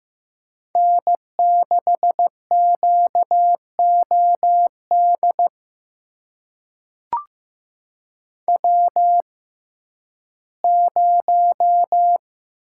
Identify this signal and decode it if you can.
Morse code, heard as a keyed tone, about 15 wpm, spaced as for 11 wpm, 700 Hz. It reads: N6QOD W0
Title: USA Call Signs - 5 Characters - 6x Character Spacing - 15wpm